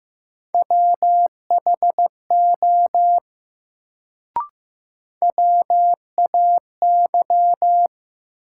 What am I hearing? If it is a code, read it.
Morse code, heard as a keyed tone, about 15 wpm, 700 Hz. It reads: WHO WAY
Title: Sentences from Top 100 Words - New Word Drill - 15wpm